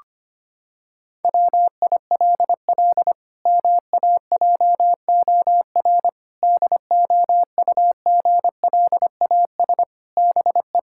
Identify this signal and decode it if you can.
Morse code, heard as a keyed tone, about 25 wpm, 700 Hz. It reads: WILL MAJOR DOUGLAS BE
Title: Pangram Challenge v2 - 25wpm